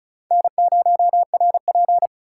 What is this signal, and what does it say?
Morse code, heard as a keyed tone, about 36 wpm, 700 Hz. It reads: N0RP
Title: USA Call Signs - 4 Characters - 2x Character Spacing - 35wpm